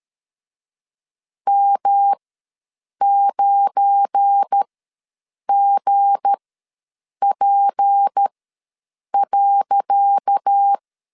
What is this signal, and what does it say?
Morse code, heard as a keyed tone, about 13 wpm, spaced as for 7 wpm, 800 Hz. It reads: M9GP.
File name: LEZIONI / 10 SIGNS / 10C DOT ws 12wpm 3548